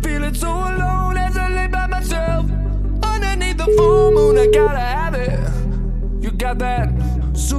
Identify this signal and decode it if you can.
Ringback tone (Europe, 425 Hz), with noise.